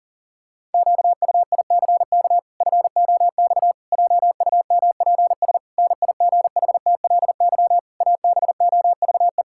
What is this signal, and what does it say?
Morse code, heard as a keyed tone, about 40 wpm, 700 Hz. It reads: QUICK FOX JUMPS NIGHTLY ABOVE
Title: Pangram Challenge v2 - 40wpm